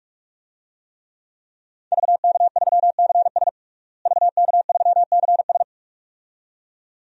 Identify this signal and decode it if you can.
Morse code, heard as a keyed tone, about 42 wpm, 700 Hz. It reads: VK3CS VK3CS
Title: DXCC-100 Call Signs - 5 Characters - Repeated 2x - 45wpm